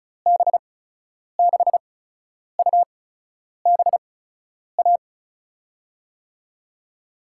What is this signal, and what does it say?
Morse code, heard as a keyed tone, about 36 wpm, spaced as for 10 wpm, 700 Hz. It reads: B6UBA
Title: USA Call Signs - 6 Characters - 8x Character Spacing - 35wpm